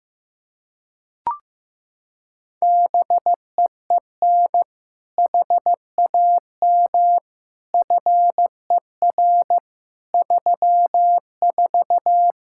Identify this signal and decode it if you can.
Morse code, heard as a keyed tone, about 15 wpm, 700 Hz. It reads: BEEN HAM FER 34
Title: Practice QSO - Exchange 3 - More Info - 15wpm